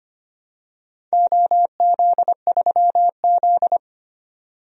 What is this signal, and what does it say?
Morse code, heard as a keyed tone, about 25 wpm, 700 Hz. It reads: OZ3Z
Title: DXCC-100 Call Signs - 4 Characters - 25wpm